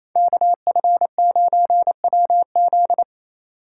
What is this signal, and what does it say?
Morse code, heard as a keyed tone, about 28 wpm, 700 Hz. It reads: KF9WZ
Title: USA Call Signs - 5 Characters - 4x Character Spacing - 28wpm